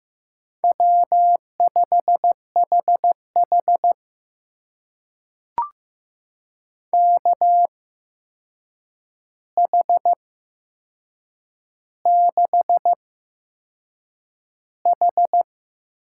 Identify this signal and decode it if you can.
Morse code, heard as a keyed tone, about 15 wpm, spaced as for 6 wpm, 700 Hz. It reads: W5HH KH6H
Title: USA Call Signs - 4 Characters - 8x Character Spacing - 15wpm